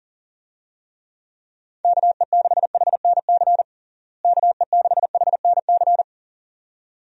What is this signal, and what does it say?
Morse code, heard as a keyed tone, about 40 wpm, 700 Hz. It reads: KE6HNC KE6HNC
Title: USA Call Signs - 6 Characters - Repeated 2x - 40wpm